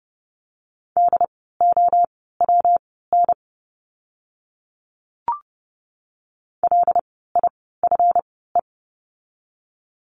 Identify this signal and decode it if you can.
Morse code, heard as a keyed tone, about 30 wpm, spaced as for 17 wpm, 700 Hz. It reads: DOWN LIFE
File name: course / 30:10wpm / 237 Lesson 38f - Letter q - Words 30wpm